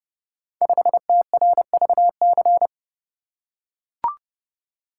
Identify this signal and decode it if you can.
Morse code, heard as a keyed tone, about 29 wpm, 700 Hz. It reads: 5TRVC
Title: Code Groups - 5 Letter:Numbers - 30wpm